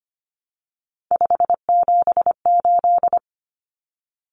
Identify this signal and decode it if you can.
Morse code, heard as a keyed tone, about 25 wpm, 700 Hz. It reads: 578
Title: Numbers - 3-Digit 25wpm